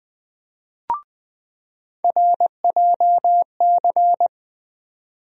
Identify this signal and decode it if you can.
Morse code, heard as a keyed tone, about 20 wpm, 700 Hz. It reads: RJC
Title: USA Call Sign Postfixes - 3 Letters 20wpm